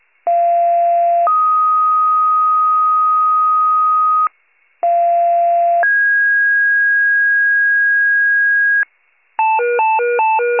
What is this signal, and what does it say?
Two-tone page, heard as a one-second tone, then a three-second tone, about 690 then 1250 Hz, about 690 then 1650 Hz, with noise. It is heard from 0.3 to 8.8 s.
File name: wcalert